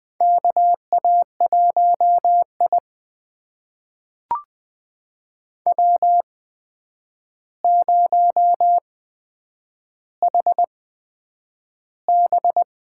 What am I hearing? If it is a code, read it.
Morse code, heard as a keyed tone, about 20 wpm, spaced as for 9 wpm, 700 Hz. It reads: KA1I W0HB
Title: USA Call Signs - 8x Character Spacing - 20wpm